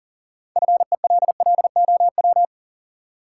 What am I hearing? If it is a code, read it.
Morse code, heard as a keyed tone, about 40 wpm, 700 Hz. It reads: FELLOW
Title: Sets of 5 Words Encore - New Word Drill - 40wpm